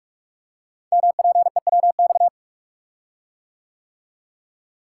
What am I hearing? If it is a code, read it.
Morse code, heard as a keyed tone, about 44 wpm, 690 Hz. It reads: MPEWX
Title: Code Groups - 5 Letters - 45wpm